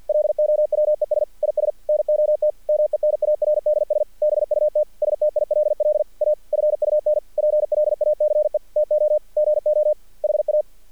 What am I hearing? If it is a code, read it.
Morse code, heard as a keyed tone, about 46 wpm, 610 Hz. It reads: LOWES IS NOT MENARDS BUT STILL A FUN PLACE TO GO HA
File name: archived / CRSnet